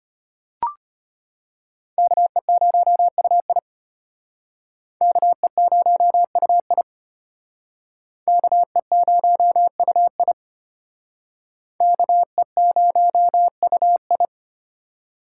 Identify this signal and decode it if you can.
Morse code, heard as a keyed tone, about 32 wpm, 700 Hz. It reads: KE0UI KE0UI KE0UI KE0UI